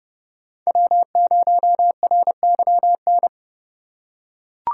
Morse code, heard as a keyed tone, about 30 wpm, 700 Hz. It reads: W0RYN
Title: USA Call Signs - 5 Characters - 3x Character Spacing - 30wpm